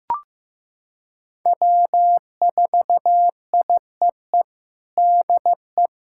Morse code, heard as a keyed tone, about 15 wpm, 700 Hz. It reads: W4IEE DE
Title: USA Callsign DE Callsign - 5 Characters - 15wpm